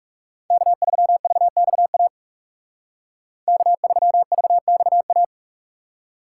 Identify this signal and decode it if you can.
Morse code, heard as a keyed tone, about 44 wpm, 700 Hz. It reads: K3VXA K3VXA